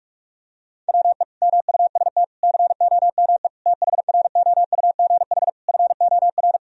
Morse code, heard as a keyed tone, about 41 wpm, 700 Hz. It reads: WE MUST COME THROUGH FOR
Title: Sentences from Top 1000 Words - Review Part 1 - 45wpm